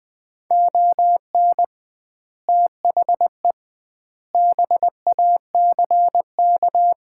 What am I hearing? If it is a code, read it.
Morse code, heard as a keyed tone, about 21 wpm, 700 Hz. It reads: ON THE BACK
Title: 2x Word-Spacing - Sets of 4 Words - 20wpm